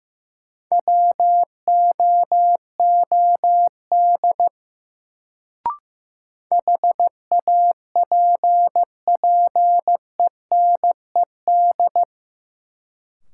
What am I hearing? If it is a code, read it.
Morse code, heard as a keyed tone, about 15 wpm, 700 Hz. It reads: WOOD HAPPENED
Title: Sentences from Top 1000 Words - Review Part 2 - New Word Drill - 15wpm